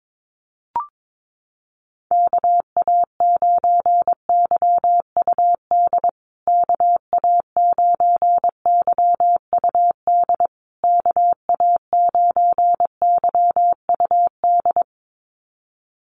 Morse code, heard as a keyed tone, about 22 wpm, 700 Hz. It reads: KA9YUD KA9YUD KA9YUD
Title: USA Call Signs Repeated 3x 22wpm